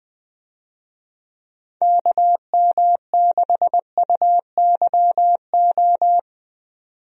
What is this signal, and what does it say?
Morse code, heard as a keyed tone, about 20 wpm, 700 Hz. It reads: KM6UYO
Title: USA Call Signs - 6 Characters - 20wpm